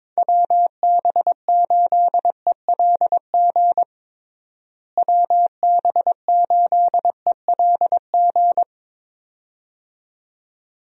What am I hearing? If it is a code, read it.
Morse code, heard as a keyed tone, about 22 wpm, 700 Hz. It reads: WB8ELG WB8ELG
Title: USA Call Signs - 6 Characters - Repeated 2x - 22wpm